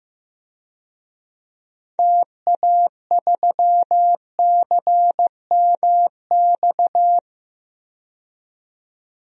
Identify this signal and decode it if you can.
Morse code, heard as a keyed tone, about 15 wpm, 700 Hz. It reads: TA3CMX
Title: DXCC-100 Call Signs - 6 Characters - 15wpm